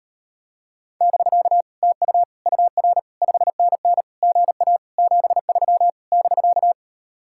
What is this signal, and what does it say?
Morse code, heard as a keyed tone, about 38 wpm, 700 Hz. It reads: <BK> TU UR 5NN GA 73 <BK>